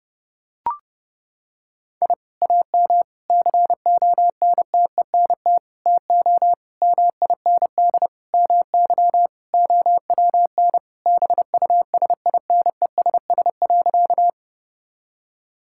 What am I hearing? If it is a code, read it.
Morse code, heard as a keyed tone, about 30 wpm, 700 Hz. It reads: I AM CONTENT TO MIND MY OWN BUSINESS.
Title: Sentences from Top 1000 Words - Review Part 6 - 30wpm